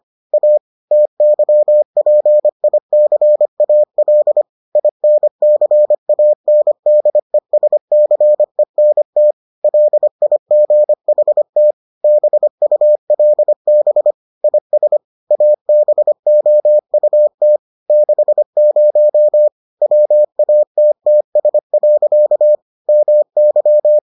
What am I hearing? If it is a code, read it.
Morse code, heard as a keyed tone, about 25 wpm, 600 Hz. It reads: A TYPICAL INCANDESCENT LIGHT BULB IS ABOUT 60 WATTS. MY